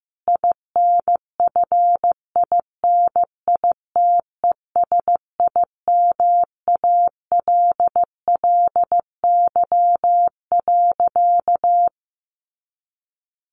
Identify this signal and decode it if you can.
Morse code, heard as a keyed tone, about 15 wpm, 700 Hz. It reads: INFINITESIMALLY.
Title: A Sound Of Thunder 15wpm